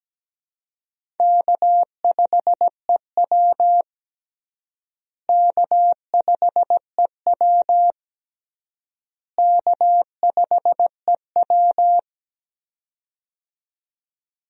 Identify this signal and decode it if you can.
Morse code, heard as a keyed tone, about 17 wpm, 700 Hz. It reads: K5EW K5EW K5EW
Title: USA Call Signs - 4 Characters - Repeated 3x - 17wpm